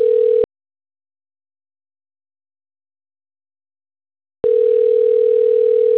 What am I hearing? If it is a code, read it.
Ringback tone (North America), 440 + 480 Hz.